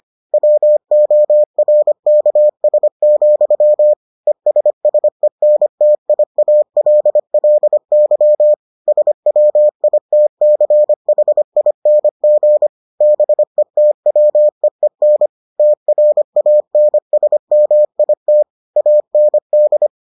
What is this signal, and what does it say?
Morse code, heard as a keyed tone, about 25 wpm, 600 Hz. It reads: WORKS, ESSENTIALLY SWITCHING BETWEEN TRANSMIT AND